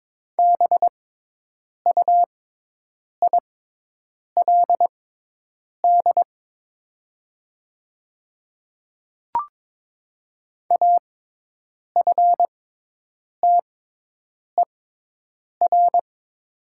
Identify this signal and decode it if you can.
Morse code, heard as a keyed tone, about 22 wpm, spaced as for 8 wpm, 700 Hz. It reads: BUILD AFTER
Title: 6x Character Spacing - Top 200 Words - 22wpm